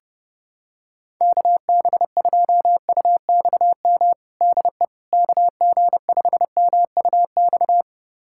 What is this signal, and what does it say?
Morse code, heard as a keyed tone, about 29 wpm, 700 Hz. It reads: KB2UXM DE KG5MUX
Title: USA Callsign DE Callsign - 6 Characters - 30wpm